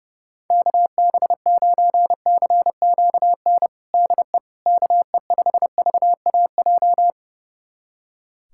Morse code, heard as a keyed tone, about 32 wpm, 700 Hz. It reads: KB9CQN DE KE5VAJ
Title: USA Callsign DE Callsign - 6 Characters - 30wpm